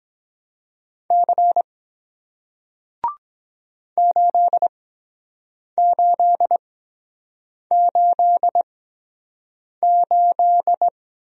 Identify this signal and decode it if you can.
Morse code, heard as a keyed tone, about 23 wpm, spaced as for 7 wpm, 700 Hz. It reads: C 8888